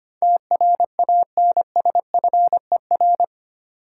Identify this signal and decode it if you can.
Morse code, heard as a keyed tone, about 25 wpm, 700 Hz. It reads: TRANSFER
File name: Eight Letter Words - 25wpm